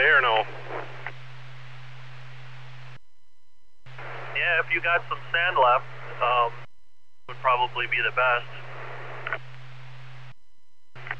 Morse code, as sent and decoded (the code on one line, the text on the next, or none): none